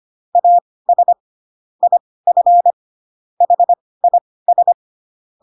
Morse code, heard as a keyed tone, about 25 wpm, spaced as for 18 wpm, 700 Hz. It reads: AS IF HIS
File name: Pinocchio / Pinocchio ch6 12 25 ik1hge